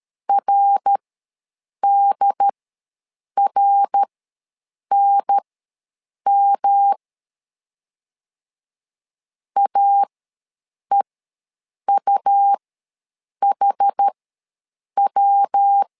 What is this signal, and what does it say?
Morse code, heard as a keyed tone, about 13 wpm, spaced as for 7 wpm, 800 Hz. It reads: RDRNM AEUHW
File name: LEZIONI / 3 MWRO / 3 MWR ws 12wpm 1809